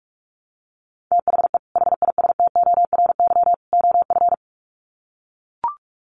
Morse code, heard as a keyed tone, about 44 wpm, 700 Hz. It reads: THE HISTORY OF
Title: Sets of 3 Words 45wpm